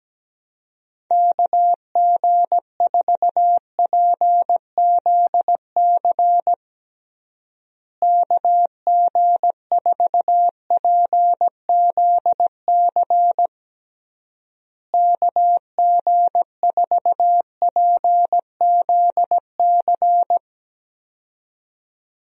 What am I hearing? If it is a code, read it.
Morse code, heard as a keyed tone, about 17 wpm, 700 Hz. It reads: KG4PZC KG4PZC KG4PZC
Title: USA Call Signs - 6 Characters - Repeated 3x - 17wpm